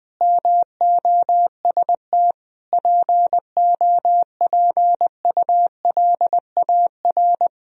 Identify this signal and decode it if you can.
Morse code, heard as a keyed tone, about 20 wpm, 700 Hz. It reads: MOST POPULAR
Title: Speed-Racing - Sets of 5 Words 20wpm